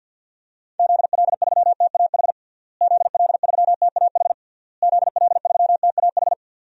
Morse code, heard as a keyed tone, about 52 wpm, 700 Hz. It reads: ZL3TAH ZL3TAH ZL3TAH
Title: DXCC-100 Call Signs - 6 Characters - Repeated 3x - 50wpm